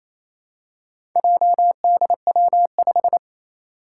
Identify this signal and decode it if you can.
Morse code, heard as a keyed tone, about 28 wpm, 700 Hz. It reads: JDW5
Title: Four Challenging Letter-Number Combinations 28wpm